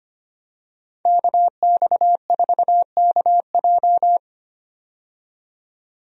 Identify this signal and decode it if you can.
Morse code, heard as a keyed tone, about 25 wpm, 700 Hz. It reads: KX4KJ
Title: USA Call Signs - 5 Characters - 25wpm